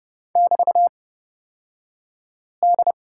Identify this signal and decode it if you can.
Morse code, heard as a keyed tone, about 31 wpm, spaced as for 5 wpm, 700 Hz. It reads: <BT>D
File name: Lesson 37d - Letter x - Two Characters 30wpm